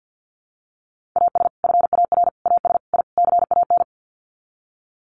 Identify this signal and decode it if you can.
Morse code, heard as a keyed tone, about 49 wpm, 700 Hz. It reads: AS FAR AS I CAN
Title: Sets of 5 Words Encore 50wpm